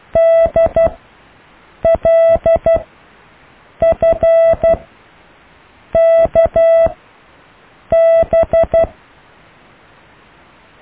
Morse code, heard as a keyed tone, about 12 wpm, spaced as for 6 wpm, 660 Hz. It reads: DLFKB
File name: cw / IJLUM+CDRO+EKTPV+BFHNW